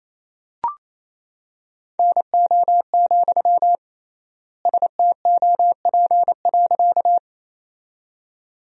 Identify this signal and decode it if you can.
Morse code, heard as a keyed tone, about 28 wpm, 700 Hz. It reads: NO, STOP.
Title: 3x Word-Spacing - Sentences from Top 300 Words - 28wpm